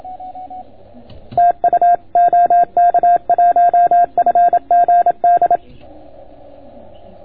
Morse code, heard as a keyed tone, about 27 wpm, 690 Hz. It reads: TU OK1FGD